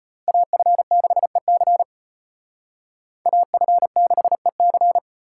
Morse code, heard as a keyed tone, about 32 wpm, 700 Hz. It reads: AF6EC AF6EC